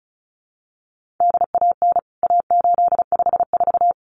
Morse code, heard as a keyed tone, about 34 wpm, 700 Hz. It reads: DAN A854
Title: CWT Exchange 35wpm